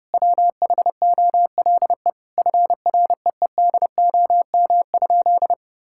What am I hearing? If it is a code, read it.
Morse code, heard as a keyed tone, about 30 wpm, 700 Hz. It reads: WHOLE FREEDOM?
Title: Sentences from Top 1000 Words - Review Part 5 - 30wpm